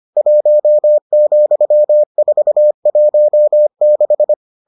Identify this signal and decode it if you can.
Morse code, heard as a keyed tone, about 25 wpm, 600 Hz. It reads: 1,416